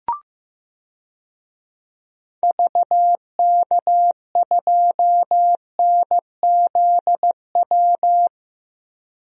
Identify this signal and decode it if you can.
Morse code, heard as a keyed tone, about 15 wpm, 700 Hz. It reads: VK2NZW
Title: DXCC-100 Call Signs - 6 Characters - 15wpm